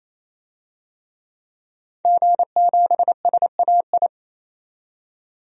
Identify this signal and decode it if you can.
Morse code, heard as a keyed tone, about 27 wpm, 700 Hz. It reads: G7SAI
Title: DXCC-100 Call Signs - 5 Characters - 28wpm